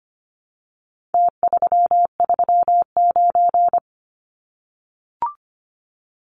Morse code, heard as a keyed tone, about 25 wpm, 700 Hz. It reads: T339
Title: Numbers - Contest Style - 4-Digit 25wpm